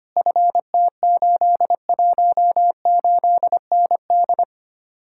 Morse code, heard as a keyed tone, about 32 wpm, 700 Hz. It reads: FT818ND